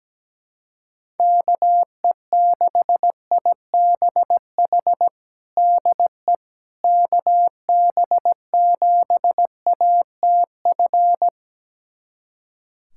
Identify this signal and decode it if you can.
Morse code, heard as a keyed tone, about 17 wpm, 700 Hz. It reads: KE6IBH DE KB7ATF
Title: USA Callsign DE Callsign - 6 Characters - 17wpm